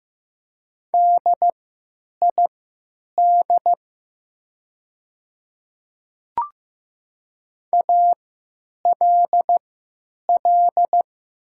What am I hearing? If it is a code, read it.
Morse code, heard as a keyed tone, about 15 wpm, spaced as for 9 wpm, 700 Hz. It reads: DID ALL
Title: 3x Character Spacing - Top 100 Words - 15wpm